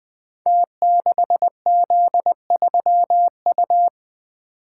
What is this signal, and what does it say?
Morse code, heard as a keyed tone, about 20 wpm, 700 Hz. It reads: T6Z3U
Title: Code Groups - 5 Letter:Numbers - 20wpm